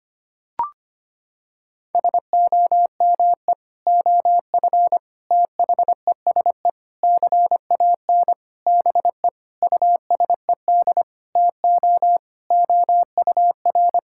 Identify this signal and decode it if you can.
Morse code, heard as a keyed tone, about 25 wpm, 700 Hz. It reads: SOME OF THESE CAN BE USED TO OUR
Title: Sentences from Top 1000 Words - Review Part 6 - 25wpm